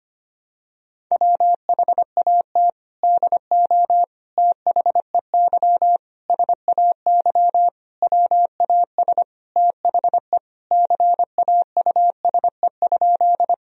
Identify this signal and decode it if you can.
Morse code, heard as a keyed tone, about 25 wpm, 700 Hz. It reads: WHAT DO THEY SAY WAS THE CAUSE?
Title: Sentences from Top 1000 Words - Review Part 1 - 25wpm